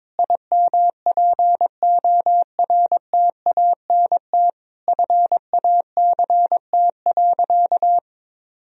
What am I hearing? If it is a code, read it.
Morse code, heard as a keyed tone, about 22 wpm, 700 Hz. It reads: IMPORTANT FACT.
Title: Sentences from Top 1000 Words - Review Part 3 - 22wpm